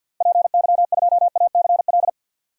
Morse code, heard as a keyed tone, about 50 wpm, 700 Hz. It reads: PY2ACL
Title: DXCC-100 Call Signs - 6 Characters - 50wpm